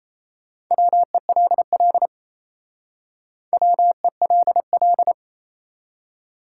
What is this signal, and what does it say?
Morse code, heard as a keyed tone, about 37 wpm, 700 Hz. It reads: WELL WELL